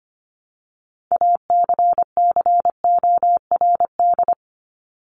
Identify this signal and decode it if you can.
Morse code, heard as a keyed tone, about 25 wpm, 700 Hz. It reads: ACCORD
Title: ING Suffix - Top 100 Words with Base Words 25wpm